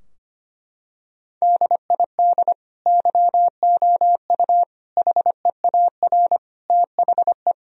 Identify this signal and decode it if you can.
Morse code, heard as a keyed tone, about 25 wpm, 700 Hz. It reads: DID YOU HEAR THE
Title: Sentences from Top 1000 Words - Review Part 2 - 25wpm